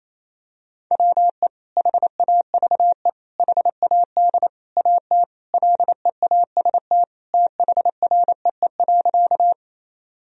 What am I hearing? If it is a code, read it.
Morse code, heard as a keyed tone, about 28 wpm, 700 Hz. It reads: WE HAVE HAD AT LEAST THREE.
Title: Sentences from Top 1000 Words - Review Part 1 - 28wpm